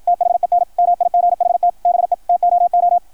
Morse code, heard as a keyed tone, about 53 wpm, 700 Hz. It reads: THEN MIGHT BE TOO
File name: archived / CRSnet